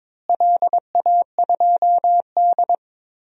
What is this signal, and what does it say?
Morse code, heard as a keyed tone, about 22 wpm, 700 Hz. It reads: LA2D
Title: DXCC-100 Call Signs - 4 Characters - Repeated 3x - 22wpm